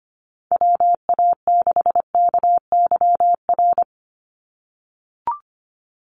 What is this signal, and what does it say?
Morse code, heard as a keyed tone, about 25 wpm, 700 Hz. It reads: WA6KYR
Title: NA SOTA Chaser Callsigns 25wpm